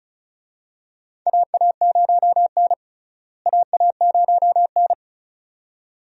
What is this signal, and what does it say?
Morse code, heard as a keyed tone, about 36 wpm, 700 Hz. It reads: AA0N AA0N
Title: USA Call Signs - 4 Characters - Repeated 2x - 35wpm